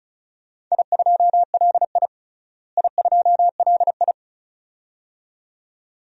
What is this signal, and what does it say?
Morse code, heard as a keyed tone, about 36 wpm, 700 Hz. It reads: I2LI I2LI
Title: DXCC-100 Call Signs - 4 Characters - Repeated 2x - 35wpm